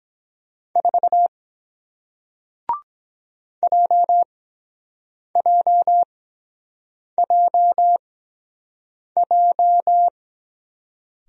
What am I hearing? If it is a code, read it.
Morse code, heard as a keyed tone, about 22 wpm, spaced as for 7 wpm, 700 Hz. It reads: 4 JJJJ